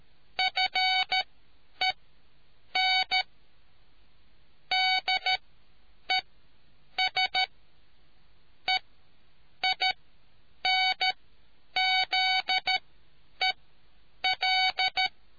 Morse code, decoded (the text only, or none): FENDES EINZEL